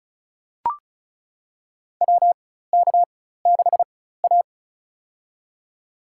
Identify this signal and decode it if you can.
Morse code, heard as a keyed tone, about 36 wpm, spaced as for 16 wpm, 700 Hz. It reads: WK6A
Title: USA Call Signs - 4x Character Spacing - 35wpm